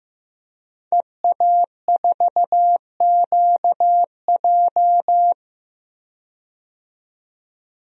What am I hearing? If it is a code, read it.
Morse code, heard as a keyed tone, about 15 wpm, 700 Hz. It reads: EA4QJ
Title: DXCC-100 Call Signs - 5 Characters - 15wpm